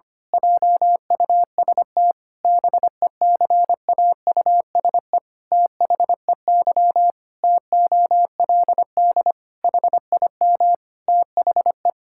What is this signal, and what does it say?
Morse code, heard as a keyed tone, about 25 wpm, 700 Hz. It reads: JUST BECAUSE THEY TOLD HIM THE
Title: Pangram Challenge v2 - 25wpm